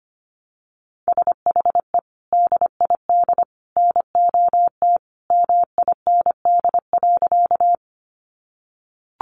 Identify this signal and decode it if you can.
Morse code, heard as a keyed tone, about 25 wpm, 700 Hz. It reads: SHE DID NOT MIND.
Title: Sentences from Top 600 Words 25wpm